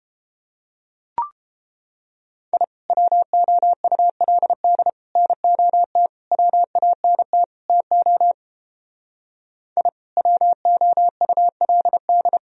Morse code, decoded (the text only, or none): I WOULD NOT WANT TO I WOULD